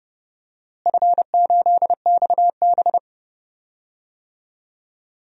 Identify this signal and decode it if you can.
Morse code, heard as a keyed tone, about 29 wpm, 700 Hz. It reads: F8XB
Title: DXCC-100 Call Signs - 4 Characters - Repeated 3x - 30wpm